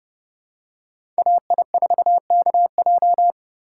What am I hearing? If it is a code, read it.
Morse code, heard as a keyed tone, about 30 wpm, 700 Hz. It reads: AI4KJ
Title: USA Call Signs - 5 Characters - 2x Character Spacing - 30wpm